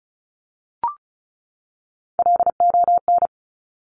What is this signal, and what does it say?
Morse code, heard as a keyed tone, about 36 wpm, 700 Hz. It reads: LON
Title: Top Three Letter Combinations in Words 35wpm